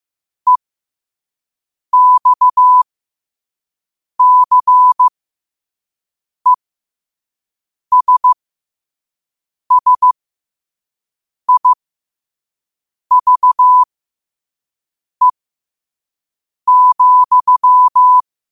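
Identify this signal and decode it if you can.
Morse code, heard as a keyed tone, about 15 wpm, spaced as for 5 wpm, 1000 Hz. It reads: EXCESSIVE,
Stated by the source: Morse code practice at 5 wpm.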